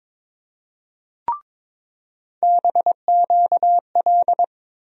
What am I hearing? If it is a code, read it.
Morse code, heard as a keyed tone, about 22 wpm, 700 Hz. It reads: BQL
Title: USA Call Sign Postfixes - 3 Letters 22wpm